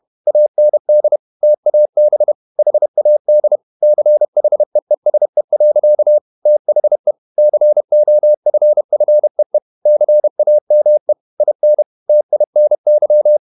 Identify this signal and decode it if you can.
Morse code, heard as a keyed tone, about 31 wpm, 600 Hz. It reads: AND TAB HAD CHEESE. THE COFFEE CAME IN TINY
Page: Never 31wpm | CW med Gnister